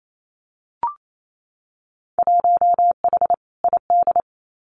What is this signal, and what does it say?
Morse code, heard as a keyed tone, about 28 wpm, 700 Hz. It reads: 1H ID
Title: Winter Field Day Exchanges 28wpm